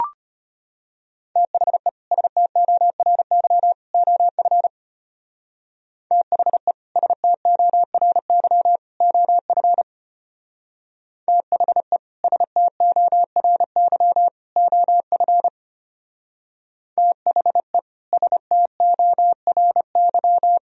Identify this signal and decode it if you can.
Morse code, heard as a keyed tone, about 31 wpm, 700 Hz. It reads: THE STORY OF THE STORY OF THE STORY OF THE STORY
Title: Speed-Racing - Sets of 3 Words 25wpm